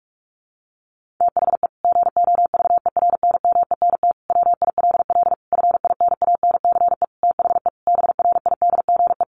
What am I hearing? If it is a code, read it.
Morse code, heard as a keyed tone, about 44 wpm, 700 Hz. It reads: THE GOVERNMENT WILL FINANCE THE BRIDGE
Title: Sentences from Top 1500 Words 45wpm